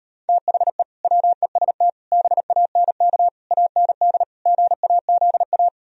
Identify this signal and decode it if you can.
Morse code, heard as a keyed tone, about 33 wpm, 700 Hz. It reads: THE WEST BANK AND GAZA